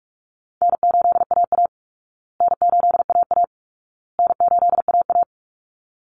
Morse code, heard as a keyed tone, about 44 wpm, 700 Hz. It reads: N8AA N8AA N8AA
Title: Call Signs Repeated 2x - 45wpm